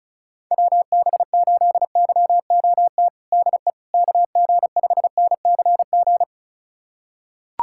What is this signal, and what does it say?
Morse code, heard as a keyed tone, about 34 wpm, 700 Hz. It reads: WB8YOT DE KG5NCG
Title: USA Callsign DE Callsign 35wpm